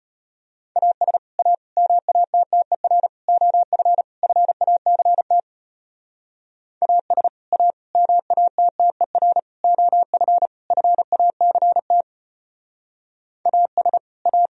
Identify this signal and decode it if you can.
Morse code, heard as a keyed tone, about 33 wpm, 700 Hz. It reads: AS A MATTER OF FACT AS A MATTER OF FACT AS A